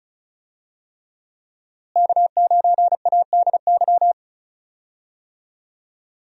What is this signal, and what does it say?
Morse code, heard as a keyed tone, about 36 wpm, 700 Hz. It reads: K9ADY
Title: USA Call Signs - 5 Characters - Repeated 3x - 35wpm